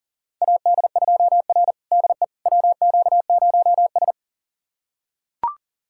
Morse code, heard as a keyed tone, about 40 wpm, 690 Hz. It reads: AD2R DE WQ0S
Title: USA Callsign DE Callsign - 4 Characters - 40wpm